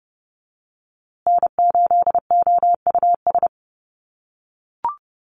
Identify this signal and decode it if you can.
Morse code, heard as a keyed tone, about 30 wpm, 700 Hz. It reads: N8OUS
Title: USA Call Signs Repeated 3x 30wpm